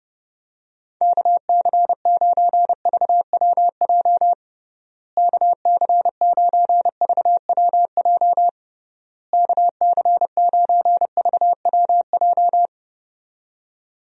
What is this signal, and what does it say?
Morse code, heard as a keyed tone, about 30 wpm, 700 Hz. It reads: KC9VWJ KC9VWJ KC9VWJ
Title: USA Call Signs - 6 Characters - Repeated 3x - 30wpm